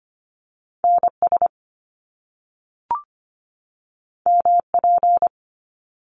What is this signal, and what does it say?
Morse code, heard as a keyed tone, about 25 wpm, 700 Hz. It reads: NS MP
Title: Top Two Letter Combinations in Words 25wpm